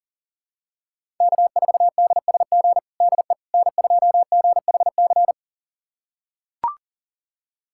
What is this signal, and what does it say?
Morse code, heard as a keyed tone, about 40 wpm, 700 Hz. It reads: K4DSG DE N2GHC
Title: USA Callsign DE Callsign - 5 Characters - 40wpm